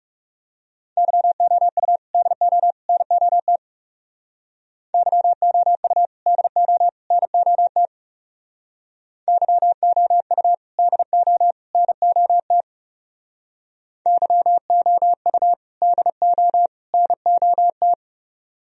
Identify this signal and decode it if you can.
Morse code, heard as a keyed tone, about 36 wpm, 700 Hz. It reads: YOU DO NOT YOU DO NOT YOU DO NOT YOU DO NOT
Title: Speed-Racing - Sets of 3 Words 30wpm